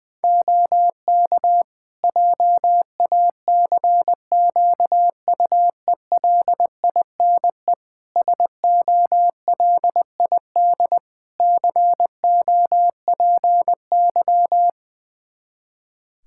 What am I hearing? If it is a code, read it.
Morse code, heard as a keyed tone, about 20 wpm, 700 Hz. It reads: OK JACQUELINE SOLID COPY
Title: K1USN SST QSO Practice 20wpm